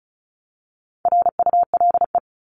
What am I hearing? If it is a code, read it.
Morse code, heard as a keyed tone, about 34 wpm, 700 Hz. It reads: RULE
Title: ES:S Suffix - Top 100 Words with Base Words 35wpm